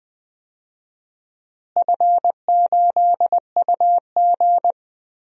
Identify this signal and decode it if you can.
Morse code, heard as a keyed tone, about 21 wpm, 700 Hz. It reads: F8UG
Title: DXCC-100 Call Signs - 4 Characters - Repeated 2x - 20wpm